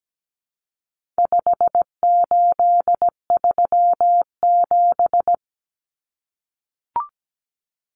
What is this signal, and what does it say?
Morse code, heard as a keyed tone, about 17 wpm, 700 Hz. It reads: 5837
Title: Numbers - 4-Digit 17wpm